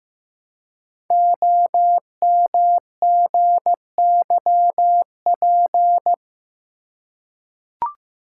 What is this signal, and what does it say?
Morse code, heard as a keyed tone, about 15 wpm, 700 Hz. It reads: OMGYP
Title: Code Groups - 5 Letters - 15wpm